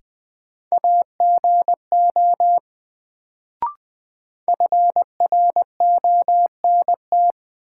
Morse code, heard as a keyed tone, about 20 wpm, 700 Hz. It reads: AGO FRONT
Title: Sentences from Top 500 Words - New Word Drill - 20wpm